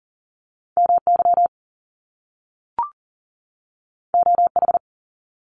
Morse code, heard as a keyed tone, about 39 wpm, 700 Hz. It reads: MY OH
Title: Top Two Letter Combinations in Words 40wpm